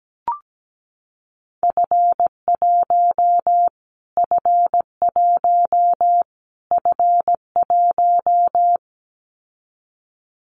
Morse code, heard as a keyed tone, about 17 wpm, 700 Hz. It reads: F1 F1 F1
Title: Callsign Prefixes for Top 25 DXCC - Repeated 3x - 17wpm